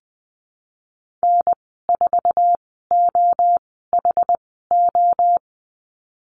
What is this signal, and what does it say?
Morse code, heard as a keyed tone, about 20 wpm, spaced as for 15 wpm, 700 Hz. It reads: N4OHO
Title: 069 Lesson 12g - Letter d - Call Signs 20wpm